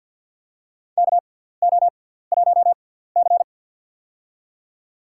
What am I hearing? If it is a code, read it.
Morse code, heard as a keyed tone, about 53 wpm, spaced as for 17 wpm, 700 Hz. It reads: KO1C
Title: USA Call Signs - 4 Characters - 6x Character Spacing - 50wpm